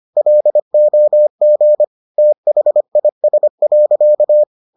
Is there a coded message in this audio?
Yes: Morse code from 0.2 to 4.4 s, heard as a keyed tone, about 25 wpm, 600 Hz. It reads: LOG THIS.